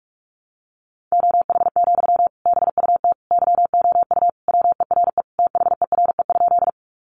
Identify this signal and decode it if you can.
Morse code, heard as a keyed tone, about 44 wpm, 700 Hz. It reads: OH, BUT YOU WERE THERE?
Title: Sentences from Top 500 Words 45wpm